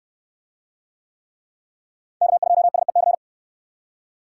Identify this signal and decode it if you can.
Morse code, heard as a keyed tone, about 59 wpm, 700 Hz. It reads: K2SEC